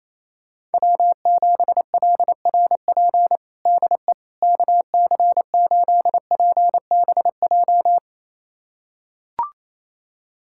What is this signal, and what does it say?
Morse code, heard as a keyed tone, about 28 wpm, 700 Hz. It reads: W7LRP DE KC8PBJ
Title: USA Callsign DE Callsign 28wpm